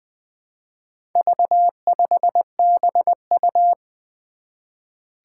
Morse code, heard as a keyed tone, about 20 wpm, 700 Hz. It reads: V5BU
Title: Four Challenging Letter-Number Combinations 20wpm